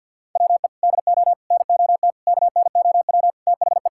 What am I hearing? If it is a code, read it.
Morse code, heard as a keyed tone, about 52 wpm, 700 Hz. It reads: WE DO NOT KNOW THE
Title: Sentences from Top 1000 Words - Review Part 6 - 50wpm